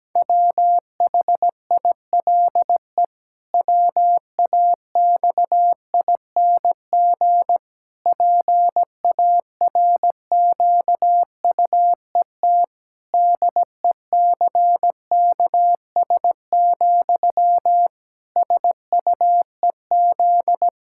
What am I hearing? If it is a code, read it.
Morse code, heard as a keyed tone, about 17 wpm, 700 Hz. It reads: WHILE WAXING PARQUET DECKS, SUEZ
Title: Pangram Challenge v2 - 17wpm